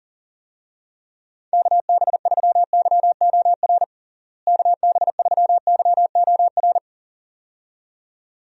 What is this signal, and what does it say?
Morse code, heard as a keyed tone, about 40 wpm, 700 Hz. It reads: KB3YOR KB3YOR
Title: USA Call Signs - 6 Characters - Repeated 2x - 40wpm